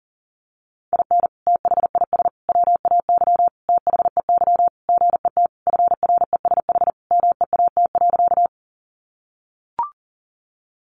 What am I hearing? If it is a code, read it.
Morse code, heard as a keyed tone, about 37 wpm, 700 Hz. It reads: IN THIS WAY THEY GET FRESH MEAT.
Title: Sentences from Top 1200 Words 40wpm